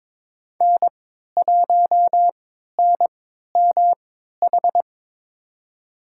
Morse code, heard as a keyed tone, about 22 wpm, spaced as for 12 wpm, 700 Hz. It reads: N1NMH
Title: USA Call Signs - 5 Characters - 3x Character Spacing - 22wpm